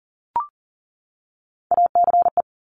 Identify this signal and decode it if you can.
Morse code, heard as a keyed tone, about 39 wpm, 690 Hz. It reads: ACE
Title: Top Three Letter Combinations in Words 40wpm